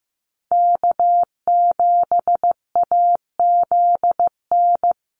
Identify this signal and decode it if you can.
Morse code, heard as a keyed tone, about 15 wpm, 700 Hz. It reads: K7AZN
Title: USA Call Signs Repeated 2x 15wpm